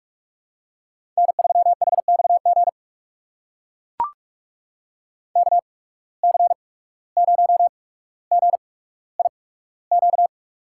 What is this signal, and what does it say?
Morse code, heard as a keyed tone, about 42 wpm, spaced as for 18 wpm, 700 Hz. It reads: N3HXG KC0GIQ
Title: USA Call Signs - 8x Character Spacing - 45wpm